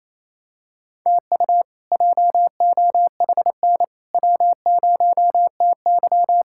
Morse code, heard as a keyed tone, about 28 wpm, 700 Hz. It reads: TU JOHN W0TY
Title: K1USN SST QSO Practice - 28wpm